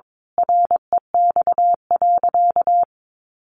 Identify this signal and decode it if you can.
Morse code, heard as a keyed tone, about 22 wpm, 700 Hz. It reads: REX.
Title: A Sound Of Thunder 22wpm